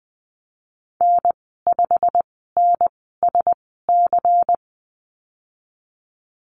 Morse code, heard as a keyed tone, about 20 wpm, spaced as for 15 wpm, 700 Hz. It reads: N5NSC
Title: 147 Lesson 24g - Letter y - Call Signs 20wpm